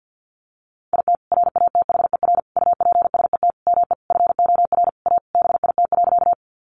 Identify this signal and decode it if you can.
Morse code, heard as a keyed tone, about 50 wpm, 700 Hz. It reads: IT RATHER UPSET ME FOR A BIT.
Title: Sentences from Top 900 Words 50wpm